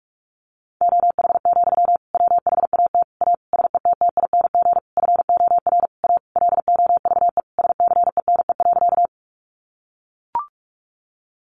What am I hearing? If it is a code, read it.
Morse code, heard as a keyed tone, about 44 wpm, 700 Hz. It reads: OH, WHAT A SETTING FOR A LOVE SCENE.
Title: Sentences from Top 1300 Words 45wpm